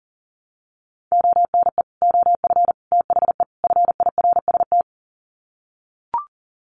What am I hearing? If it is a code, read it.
Morse code, heard as a keyed tone, about 40 wpm, 700 Hz. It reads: ONE OF THE FIRST
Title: Sets of 4 Words 40wpm